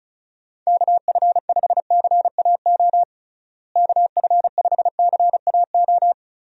Morse code, heard as a keyed tone, about 34 wpm, 700 Hz. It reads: KF5CAO KF5CAO
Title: USA Call Signs - 6 Characters - Repeated 3x - 35wpm